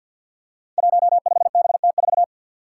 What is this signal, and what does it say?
Morse code, heard as a keyed tone, about 53 wpm, 700 Hz. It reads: 15BT4
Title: Five Challenging Letter-Number Combinations 50wpm